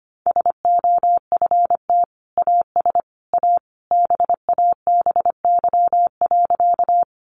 Morse code, heard as a keyed tone, about 25 wpm, 700 Hz. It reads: SOFT AS A BABY.
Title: Sentences from Top 1500 Words 25wpm